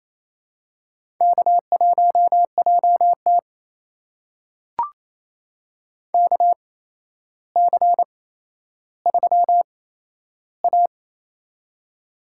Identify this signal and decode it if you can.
Morse code, heard as a keyed tone, about 29 wpm, spaced as for 12 wpm, 700 Hz. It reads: K1JT KC3A
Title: USA Call Signs - 4 Characters - 8x Character Spacing - 28wpm